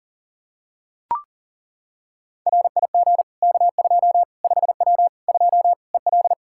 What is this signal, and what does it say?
Morse code, heard as a keyed tone, about 40 wpm, 700 Hz. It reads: RIG K2 5W 2 EL
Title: Practice QSO - Exchange 2 - Info - 40wpm